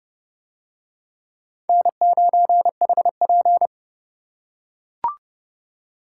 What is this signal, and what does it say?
Morse code, heard as a keyed tone, about 27 wpm, 700 Hz. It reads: N9HP